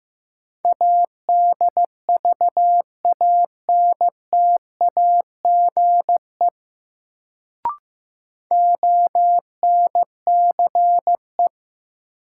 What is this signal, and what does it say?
Morse code, heard as a keyed tone, about 15 wpm, 700 Hz. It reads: ADVANTAGE ONCE
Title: Sets of 4 Words - New Word Drill - 15wpm